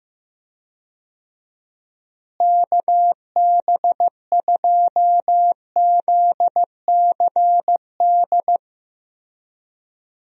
Morse code, heard as a keyed tone, about 15 wpm, 700 Hz. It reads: KB2ZCD
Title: USA Call Signs - 6 Characters - Repeated 2x - 15wpm